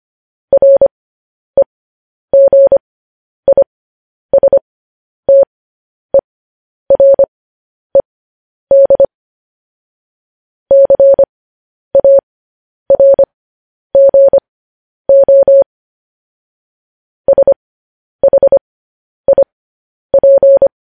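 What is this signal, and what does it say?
Morse code, heard as a keyed tone, about 25 wpm, spaced as for 10 wpm, 550 Hz. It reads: REGISTERED CARGO SHIP
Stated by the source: Morse code news headlines sent at 10.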